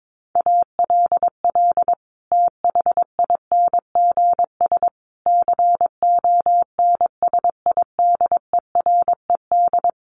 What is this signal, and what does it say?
Morse code, heard as a keyed tone, about 22 wpm, 700 Hz. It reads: ALL THINGS CONSIDERED
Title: Sets of 5 Words Encore 22wpm